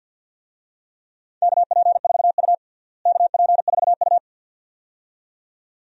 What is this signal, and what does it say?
Morse code, heard as a keyed tone, about 50 wpm, 700 Hz. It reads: KP4U KP4U
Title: DXCC-100 Call Signs - 4 Characters - Repeated 2x - 50wpm